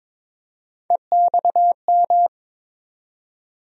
Morse code, heard as a keyed tone, about 22 wpm, 700 Hz. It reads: EXM
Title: USA Call Sign Postfixes - 3 Letters 22wpm